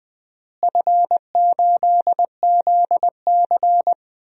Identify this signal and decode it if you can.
Morse code, heard as a keyed tone, about 21 wpm, 700 Hz. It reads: F8ZC
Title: DXCC-100 Call Signs - 4 Characters - Repeated 3x - 20wpm